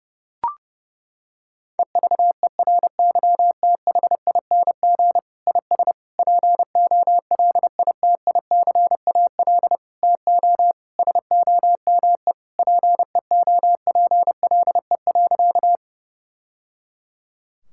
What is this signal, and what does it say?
Morse code, heard as a keyed tone, about 32 wpm, 700 Hz. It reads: EVERYTHING IS POLITICAL TO SOME PEOPLE.
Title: Sentences from Top 1000 Words - Review Part 5 - 30wpm